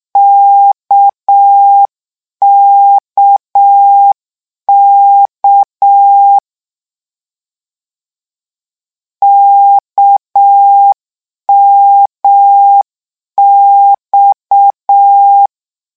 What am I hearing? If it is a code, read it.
Morse code, heard as a keyed tone, about 6 wpm, 800 Hz. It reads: KKK KMX
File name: lecon 1-vitesse 6